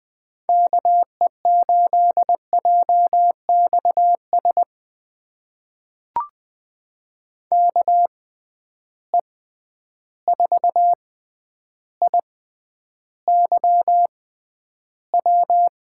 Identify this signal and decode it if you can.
Morse code, heard as a keyed tone, about 20 wpm, spaced as for 10 wpm, 700 Hz. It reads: KE8JXS KE4IYW
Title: USA Call Signs - 6 Characters - 6x Character Spacing - 20wpm